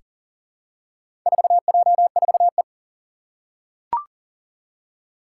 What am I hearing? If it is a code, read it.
Morse code, heard as a keyed tone, about 40 wpm, 690 Hz. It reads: 4J4E
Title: Four Challenging Letter-Number Combinations 40wpm